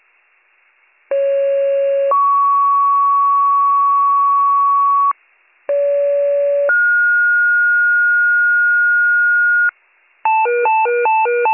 Two-tone page, heard as a one-second tone, then a three-second tone, about 550 then 1120 Hz, about 570 then 1430 Hz, with noise. It is heard from 1.1 to 9.7 s.